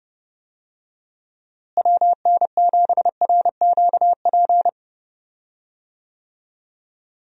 Morse code, heard as a keyed tone, about 32 wpm, 700 Hz. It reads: WN7RQP
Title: USA Call Signs - 6 Characters - Repeated 2x - 30wpm